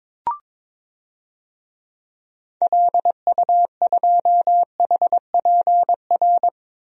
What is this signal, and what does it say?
Morse code, heard as a keyed tone, about 22 wpm, 700 Hz. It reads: LU2HPR
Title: DXCC-100 Call Signs - 6 Characters - Repeated 2x - 22wpm